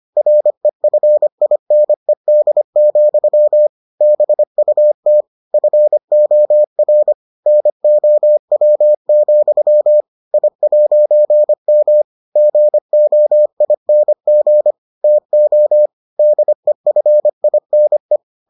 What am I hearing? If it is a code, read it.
Morse code, heard as a keyed tone, about 25 wpm, 600 Hz. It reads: REFINED, BUT FOR NOW, I'M GOING TO DEFINE